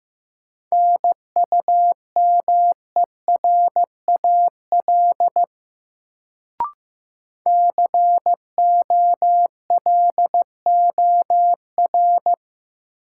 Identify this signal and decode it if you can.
Morse code, heard as a keyed tone, about 15 wpm, 700 Hz. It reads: NUMERAL COLOR
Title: Sentences from Top 1000 Words - Review Part 2 - New Word Drill - 15wpm